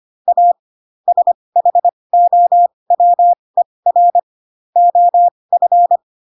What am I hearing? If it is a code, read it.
Morse code, heard as a keyed tone, about 25 wpm, spaced as for 20 wpm, 700 Hz. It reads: A SHOWER OF
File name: Pinocchio / Pinocchio ch6 15 25 ik1hge